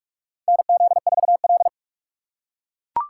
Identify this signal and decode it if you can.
Morse code, heard as a keyed tone, about 43 wpm, 700 Hz. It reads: NZ4L
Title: USA Call Signs - 4 Characters - 2x Character Spacing - 45wpm